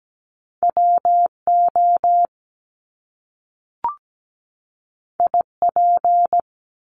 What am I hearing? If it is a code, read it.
Morse code, heard as a keyed tone, about 17 wpm, 700 Hz. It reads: WO IP
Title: Top Two Letter Combinations in Words 17wpm